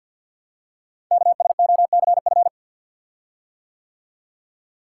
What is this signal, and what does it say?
Morse code, heard as a keyed tone, about 53 wpm, 700 Hz. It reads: KSOCF
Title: Code Groups - 5 Letter:Numbers - 50wpm